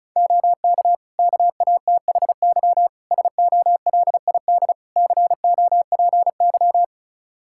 Morse code, heard as a keyed tone, about 34 wpm, 700 Hz. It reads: OK KATHY SOLID COPY
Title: Practice QSO - Exchange 3 - More Info - 35wpm